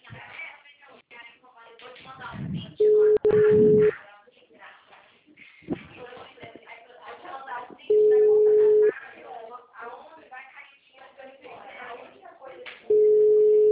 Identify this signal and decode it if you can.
Ringback tone (Europe, 425 Hz), with speech.
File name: out- -2024- - - .45541